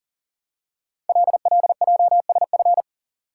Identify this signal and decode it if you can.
Morse code, heard as a keyed tone, about 39 wpm, 700 Hz. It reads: LLJSF
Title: Code Groups - 5 Letters - 40wpm